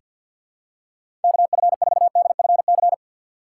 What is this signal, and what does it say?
Morse code, heard as a keyed tone, about 50 wpm, 700 Hz. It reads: KF4DFC
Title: USA Call Signs - 6 Characters - 4x Character Spacing - 50wpm